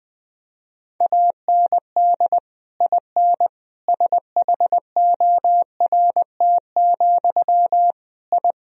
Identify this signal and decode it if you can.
Morse code, heard as a keyed tone, about 20 wpm, 700 Hz. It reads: AND IN SHORT, I
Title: Sentences from Top 1000 Words - Review Part 2 - 20wpm